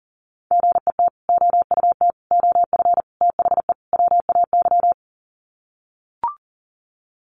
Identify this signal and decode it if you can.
Morse code, heard as a keyed tone, about 40 wpm, 700 Hz. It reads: GET OUT OF THE WAY
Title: Sets of 5 Words Encore 40wpm